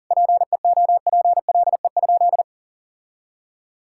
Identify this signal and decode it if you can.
Morse code, heard as a keyed tone, about 40 wpm, 700 Hz. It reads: PEOPLE?
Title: Sentences from Top 1000 Words - Review Part 5 - 40wpm